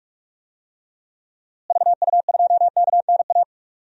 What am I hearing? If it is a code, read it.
Morse code, heard as a keyed tone, about 46 wpm, 700 Hz. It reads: VU2KNA
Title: DXCC-100 Call Signs - 6 Characters - Repeated 2x - 45wpm